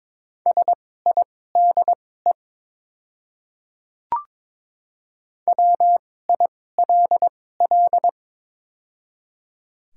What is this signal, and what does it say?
Morse code, heard as a keyed tone, about 22 wpm, spaced as for 16 wpm, 700 Hz. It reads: SIDE WILL
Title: 2x Character Spacing - Top 300 Words - 22wpm